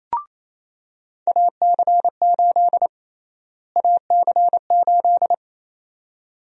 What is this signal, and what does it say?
Morse code, heard as a keyed tone, about 28 wpm, 700 Hz. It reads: AC8 AC8
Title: USA Call Sign Prefixes - Repeated 2x - 28wpm